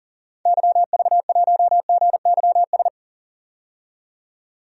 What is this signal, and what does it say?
Morse code, heard as a keyed tone, about 39 wpm, 700 Hz. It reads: YV1GYS
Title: DXCC-100 Call Signs - 6 Characters - 40wpm